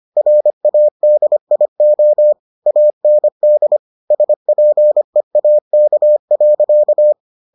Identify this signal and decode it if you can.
Morse code, heard as a keyed tone, about 25 wpm, 600 Hz. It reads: RADIO AND SPEAK.